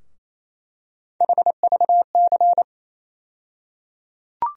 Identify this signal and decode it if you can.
Morse code, heard as a keyed tone, about 28 wpm, 700 Hz. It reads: HVC
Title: USA Call Sign Postfixes - 3 Letters 28wpm